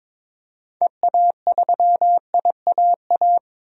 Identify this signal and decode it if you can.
Morse code, heard as a keyed tone, about 22 wpm, 700 Hz. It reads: EA3IAA
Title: DXCC-100 Call Signs - 6 Characters - 22wpm